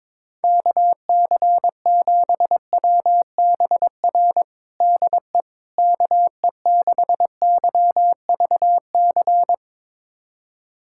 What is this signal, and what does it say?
Morse code, heard as a keyed tone, about 22 wpm, 700 Hz. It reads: KC7WBR DE KE6YVC
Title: USA Callsign DE Callsign - 6 Characters - 22wpm